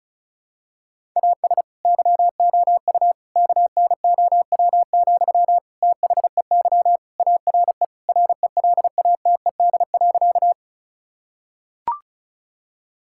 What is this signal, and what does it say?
Morse code, heard as a keyed tone, about 34 wpm, 700 Hz. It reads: AS YOU KNOW, THEY ARE RELATED.
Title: Sentences from Top 1000 Words - Review Part 5 - 35wpm